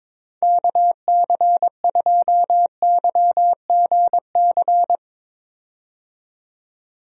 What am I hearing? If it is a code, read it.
Morse code, heard as a keyed tone, about 22 wpm, 700 Hz. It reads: KC2YGC
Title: USA Call Signs - 6 Characters - 22wpm